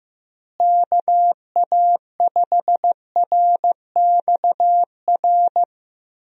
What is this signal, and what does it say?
Morse code, heard as a keyed tone, about 15 wpm, 700 Hz. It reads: KA5RXR
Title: USA Call Signs - 6 Characters - Repeated 3x - 15wpm